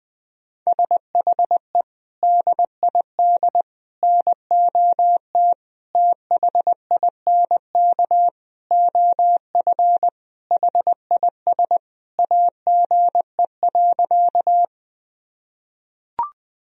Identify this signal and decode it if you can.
Morse code, heard as a keyed tone, about 20 wpm, 700 Hz. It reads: SHE DID NOT THINK OF HIS AGE.
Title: Sentences from Top 1000 Words - Review Part 3 - 20wpm